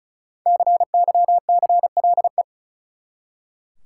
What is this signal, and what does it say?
Morse code, heard as a keyed tone, about 36 wpm, 700 Hz. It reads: CYCLE
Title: Five Letter Words - 35wpm